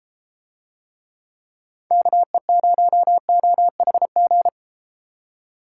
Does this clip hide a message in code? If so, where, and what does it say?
Yes: Morse code from 1.9 to 4.5 s, heard as a keyed tone, about 33 wpm, 700 Hz. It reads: KE0OHG